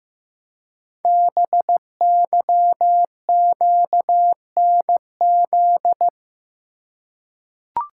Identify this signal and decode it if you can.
Morse code, heard as a keyed tone, about 15 wpm, 700 Hz. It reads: BYQNZ
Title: Code Groups - 5 Letters - 15wpm